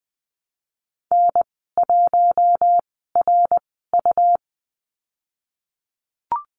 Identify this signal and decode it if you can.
Morse code, heard as a keyed tone, about 20 wpm, spaced as for 15 wpm, 700 Hz. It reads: N1RU
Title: 167 Lesson 27g - Number 7 - Call Signs 20wpm